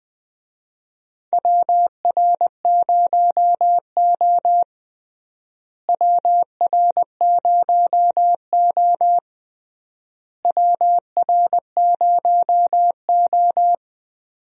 Morse code, heard as a keyed tone, about 21 wpm, 700 Hz. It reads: WR0O WR0O WR0O
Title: USA Call Signs - 4 Characters - Repeated 3x - 20wpm